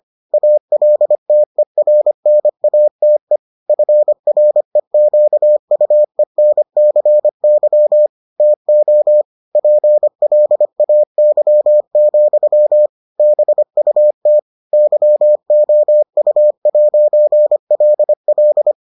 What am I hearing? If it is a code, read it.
Morse code, heard as a keyed tone, about 25 wpm, 600 Hz. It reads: ALTERNATE FREQUENCY TO PLAY, BUT YOU'LL